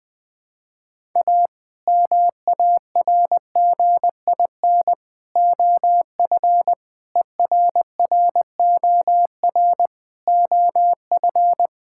Morse code, heard as a keyed tone, about 20 wpm, 700 Hz. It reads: A MARGIN OF ERROR OF
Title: Sets of 8 Words - 20wpm